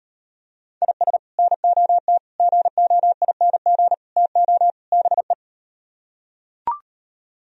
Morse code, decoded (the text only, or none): IS NOT GOING TO BE